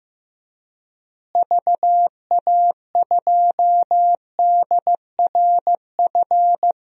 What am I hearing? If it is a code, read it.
Morse code, heard as a keyed tone, about 15 wpm, 700 Hz. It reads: VA2DRF
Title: DXCC-100 Call Signs - 6 Characters - Repeated 3x - 15wpm